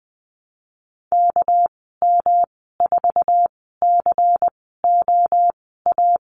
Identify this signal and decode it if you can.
Morse code, heard as a keyed tone, about 20 wpm, spaced as for 15 wpm, 700 Hz. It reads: KM4COA
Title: 238 Lesson 38g - Letter q - Call Signs 20wpm